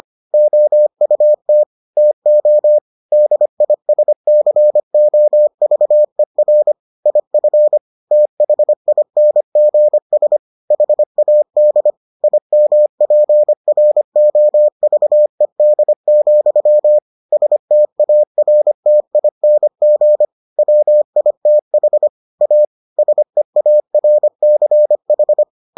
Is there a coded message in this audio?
Yes: Morse code from 0.3 to 25.4 s, heard as a keyed tone, about 25 wpm, 600 Hz. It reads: OUT TO DISCOVER IF THINGS HAD IMPROVED, STARTING WITH A SEARCH